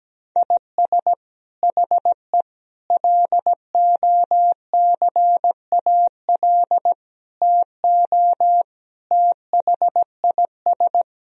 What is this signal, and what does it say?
Morse code, heard as a keyed tone, about 17 wpm, 700 Hz. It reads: IS HE LOCAL TO THIS
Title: Sentences from Top 1000 Words - Review Part 3 - 17wpm